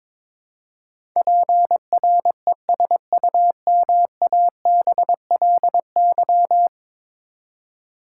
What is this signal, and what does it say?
Morse code, heard as a keyed tone, about 22 wpm, 700 Hz. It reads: PRESUMABLY
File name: practice / 22wpm / Ten Letter Words - 22wpm